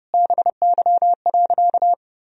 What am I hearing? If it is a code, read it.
Morse code, heard as a keyed tone, about 30 wpm, 700 Hz. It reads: BY.
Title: Sentences from Top 1000 Words - Review Part 1 - 30wpm